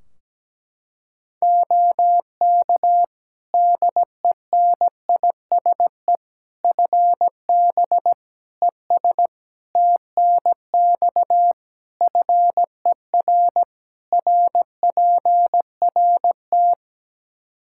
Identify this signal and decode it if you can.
Morse code, heard as a keyed tone, about 17 wpm, 700 Hz. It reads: OK DENISE FB ES TNX FER RPRT
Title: Practice QSO - Exchange 2 - Info - 17wpm